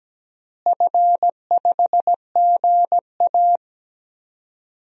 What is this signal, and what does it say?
Morse code, heard as a keyed tone, about 17 wpm, 700 Hz. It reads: F5GA
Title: DXCC-100 Call Signs - 4 Characters - Repeated 3x - 17wpm